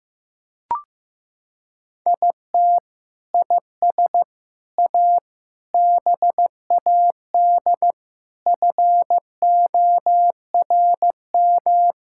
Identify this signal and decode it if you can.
Morse code, heard as a keyed tone, about 15 wpm, 700 Hz. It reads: IT IS A BAD FORM
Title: Sentences from Top 1000 Words - Review Part 5 - 15wpm